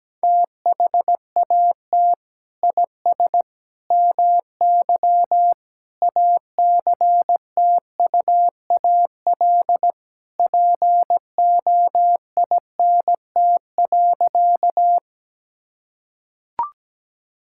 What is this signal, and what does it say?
Morse code, heard as a keyed tone, about 17 wpm, 700 Hz. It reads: THAT IS MY ACTUAL POINT.
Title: Sentences from Top 1000 Words - Review Part 5 - 17wpm